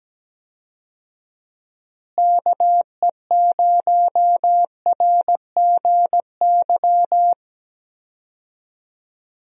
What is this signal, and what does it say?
Morse code, heard as a keyed tone, about 17 wpm, 700 Hz. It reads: KE0RGY
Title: USA Call Signs - 6 Characters - Repeated 2x - 17wpm